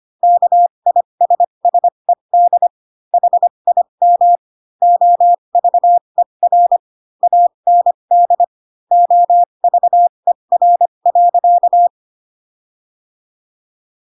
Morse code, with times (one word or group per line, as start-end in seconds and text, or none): 0.2-2.7 KISSED
3.1-4.4 HIM
4.8-6.8 OVER
7.2-8.4 AND
8.9-11.9 OVER.